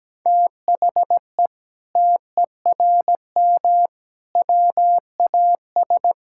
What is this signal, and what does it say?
Morse code, heard as a keyed tone, about 17 wpm, 700 Hz. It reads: THE TERM WAS
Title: Sentences from Top 1000 Words - Review Part 6 - 17wpm